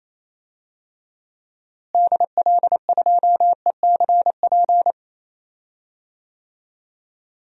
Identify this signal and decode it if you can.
Morse code, heard as a keyed tone, about 28 wpm, 700 Hz. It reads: DL2ECP
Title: DXCC-100 Call Signs - 6 Characters - Repeated 3x - 28wpm